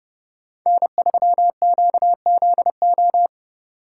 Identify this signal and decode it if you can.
Morse code, heard as a keyed tone, about 30 wpm, 700 Hz. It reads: N3QZO